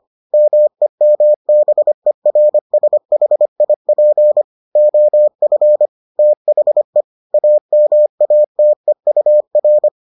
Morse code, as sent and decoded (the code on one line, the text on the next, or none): -- . -- -... .-. ... .... .. .--. / --- ..-. / - .... / .- -- .- - . ..- .-.
MEMBERSHIP OF THE AMATEUR